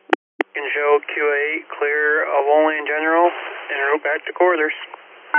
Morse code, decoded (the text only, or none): none